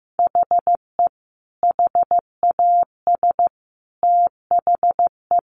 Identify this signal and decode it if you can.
Morse code, heard as a keyed tone, about 14 wpm, 700 Hz. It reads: HE HAS THE
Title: Sentences from Top 800 Words 15wpm